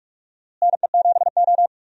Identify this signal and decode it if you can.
Morse code, heard as a keyed tone, about 43 wpm, 700 Hz. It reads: NE7O
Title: USA Call Signs - 4 Characters - Repeated 2x - 45wpm